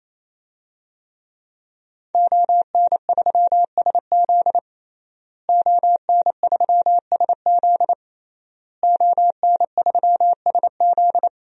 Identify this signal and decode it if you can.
Morse code, heard as a keyed tone, about 28 wpm, 700 Hz. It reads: ON3SZ ON3SZ ON3SZ
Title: DXCC-100 Call Signs - 5 Characters - Repeated 3x - 28wpm